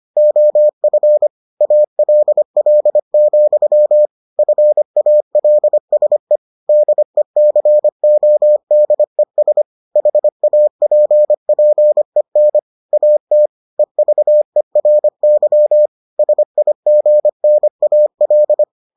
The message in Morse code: OF ALL, FALSE DECODES HAPPEN AT EVERY SIGNAL

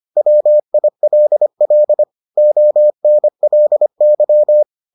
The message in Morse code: WILL ONLY

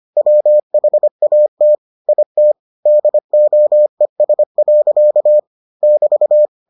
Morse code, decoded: WHAT IT DOES. <BT>

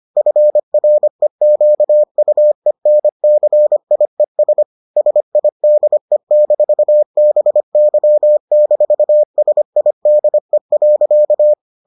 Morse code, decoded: FREQUENCIES SIDE-BY-SIDE.

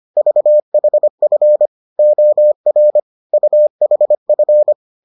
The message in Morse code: VHF OR UHF

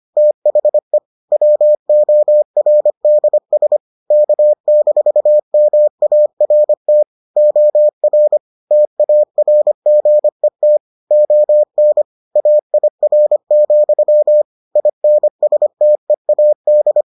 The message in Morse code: THE WORDS K-MART OR TARGET ON AIR, INSTEAD

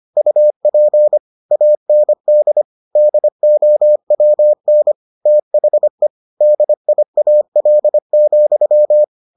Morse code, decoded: UP AND DOWN THE DIAL,

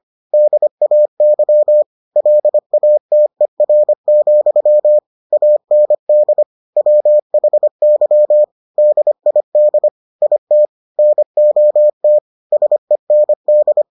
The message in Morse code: DAY LATER, AND WHY DID IT NOT SEND

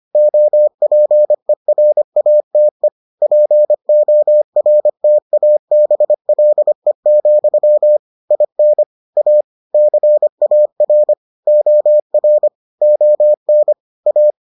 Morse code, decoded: OPERATE PORTABLE, IN A CAR OR ON A

